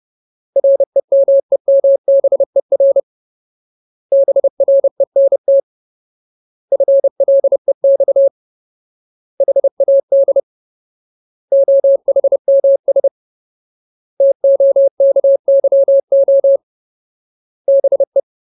Morse code, heard as a keyed tone, about 30 wpm, 550 Hz. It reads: REMEMBER BRENT FLEX HAD OHMS TOKYO BE
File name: Ultimate Word List / 30WPM Ultimate